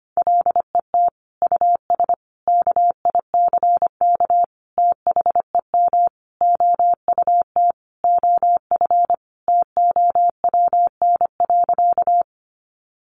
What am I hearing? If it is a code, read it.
Morse code, heard as a keyed tone, about 25 wpm, 700 Hz. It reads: LET US KICK THEM OUT OF TOWN.
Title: Sentences from Top 1300 Words 25wpm